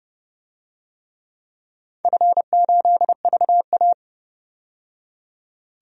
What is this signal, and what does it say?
Morse code, heard as a keyed tone, about 31 wpm, 700 Hz. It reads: F8VA
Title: DXCC-100 Call Signs - 4 Characters - 30wpm